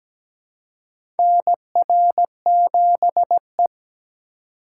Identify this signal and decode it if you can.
Morse code, heard as a keyed tone, about 17 wpm, 700 Hz. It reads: NR7E
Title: USA Call Signs - 4 Characters - 3x Character Spacing - 17wpm